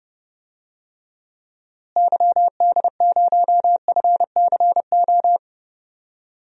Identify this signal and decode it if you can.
Morse code, heard as a keyed tone, about 30 wpm, 700 Hz. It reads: YD0FCO
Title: DXCC-100 Call Signs - 6 Characters - Repeated 2x - 30wpm